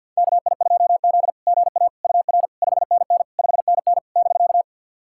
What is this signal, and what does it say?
Morse code, heard as a keyed tone, about 46 wpm, 690 Hz. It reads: KI2Z GA UR 5NN 5NN <BK>
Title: SOTA QSO Practice 50wpm